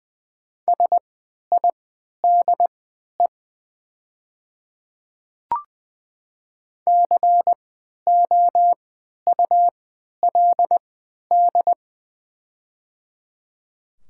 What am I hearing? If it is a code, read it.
Morse code, heard as a keyed tone, about 20 wpm, spaced as for 11 wpm, 700 Hz. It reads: SIDE COULD
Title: 3x Character Spacing - Top 100 Words - 20wpm